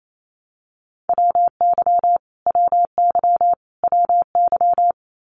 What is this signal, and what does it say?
Morse code, heard as a keyed tone, about 28 wpm, 700 Hz. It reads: WY WY WY
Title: US States Repeated 3x 28wpm